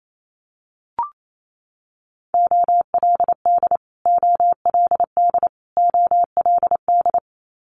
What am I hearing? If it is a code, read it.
Morse code, heard as a keyed tone, about 28 wpm, 700 Hz. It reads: OLD OLD OLD
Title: Multi-letter Phonemes Repeated 3x 28wpm